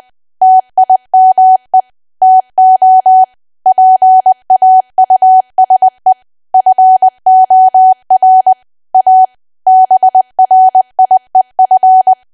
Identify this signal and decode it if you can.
Morse code, heard as a keyed tone, about 20 wpm, 750 Hz. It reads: TIME TO PAUSE FOR A BRIEF
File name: cwa / W1AW-Level3 / 20WPM